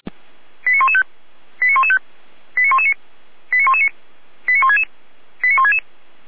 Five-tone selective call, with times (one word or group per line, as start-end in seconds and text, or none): 0.7-1.0 89115
1.6-2.0 89115
2.6-2.9 89119
3.5-3.9 89119
4.5-4.8 89166
5.4-5.8 89166